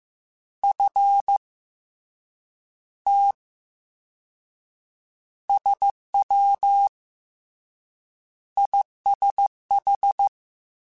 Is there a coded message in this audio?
Yes: Morse code from 0.6 to 10.3 s, heard as a keyed tone, about 15 wpm, spaced as for 9 wpm, 800 Hz. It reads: F T SWISH